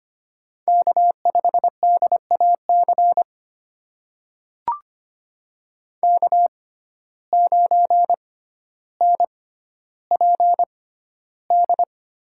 Morse code, heard as a keyed tone, about 25 wpm, spaced as for 13 wpm, 700 Hz. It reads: K5DAC K9NPD
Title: USA Call Signs - 5 Characters - 6x Character Spacing - 25wpm